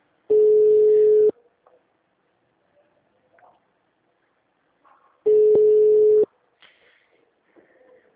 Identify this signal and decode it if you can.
Ringback tone (Europe, 425 Hz).
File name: out- -2010- - - .157